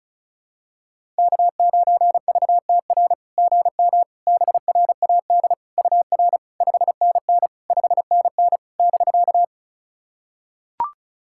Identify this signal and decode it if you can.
Morse code, heard as a keyed tone, about 36 wpm, 700 Hz. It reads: K9VTR GM BRAD UR 5NN 5NN <BK>
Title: SOTA QSO Practice 35wpm